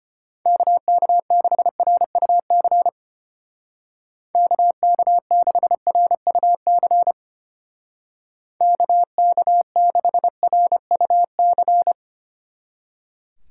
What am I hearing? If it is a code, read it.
Morse code, heard as a keyed tone, about 30 wpm, 700 Hz. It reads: KK6RUC KK6RUC KK6RUC
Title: Speed Racing - USA Call Signs 25wpm